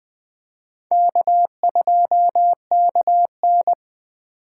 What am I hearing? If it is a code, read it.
Morse code, heard as a keyed tone, about 21 wpm, 700 Hz. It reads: K2KN